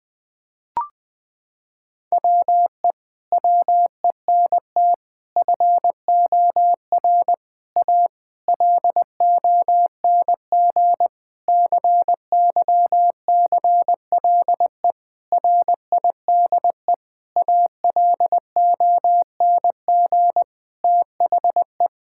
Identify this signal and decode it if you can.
Morse code, heard as a keyed tone, about 20 wpm, 700 Hz. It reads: WE WENT FOR A LONG CYCLE RIDE ALONG THE
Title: Sentences from Top 1000 Words - Review Part 6 - 20wpm